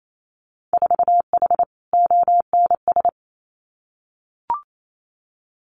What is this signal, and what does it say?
Morse code, heard as a keyed tone, about 28 wpm, 700 Hz. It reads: 4H ONS
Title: Winter Field Day Exchanges Encore 28wpm